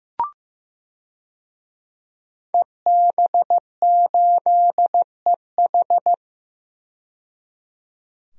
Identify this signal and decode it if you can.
Morse code, heard as a keyed tone, about 15 wpm, 700 Hz. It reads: EB8EH
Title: DXCC-100 Call Signs - 5 Characters - 15wpm